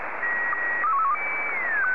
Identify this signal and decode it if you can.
SSTV (slow-scan television), Scottie DX.